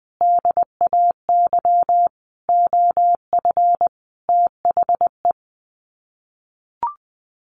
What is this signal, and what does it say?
Morse code, heard as a keyed tone, about 20 wpm, 700 Hz. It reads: DAY OF THE
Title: Sets of 5 Words Encore2 20wpm